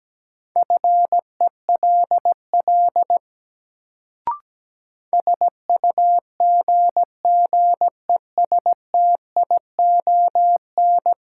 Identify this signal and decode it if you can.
Morse code, heard as a keyed tone, about 17 wpm, 700 Hz. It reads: FELL SUGGESTION
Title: Sentences from Top 1600 Words - New Word Drill Alt - 17wpm